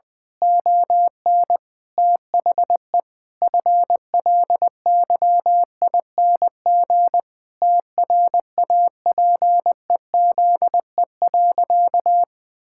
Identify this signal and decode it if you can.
Morse code, heard as a keyed tone, about 20 wpm, 700 Hz. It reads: ON THE FLYING TRAPEZE.
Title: Pangram Challenge v2 - 20wpm